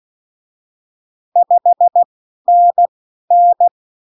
Morse code, heard as a keyed tone, about 16 wpm, spaced as for 12 wpm, 700 Hz. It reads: 5NN
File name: CWconcurso QSO libroCW4 16-8ppm